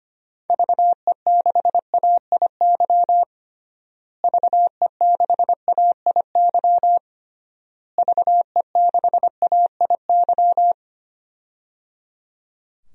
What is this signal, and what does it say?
Morse code, heard as a keyed tone, about 25 wpm, 700 Hz. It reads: VE6AIY VE6AIY VE6AIY
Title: DXCC-100 Call Signs - 6 Characters - Repeated 3x - 25wpm